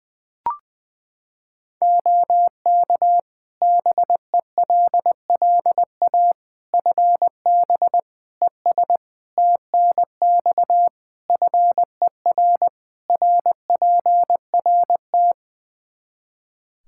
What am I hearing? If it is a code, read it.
Morse code, heard as a keyed tone, about 20 wpm, 700 Hz. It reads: OK BELLA FB ES TNX FER RPRT
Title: K1USN SST QSO Practice 20wpm